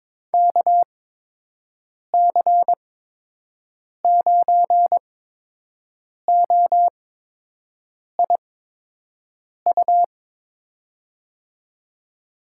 Morse code, heard as a keyed tone, about 22 wpm, spaced as for 6 wpm, 700 Hz. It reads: KC9OIU
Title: USA Call Signs - 6 Characters - 8x Character Spacing - 22wpm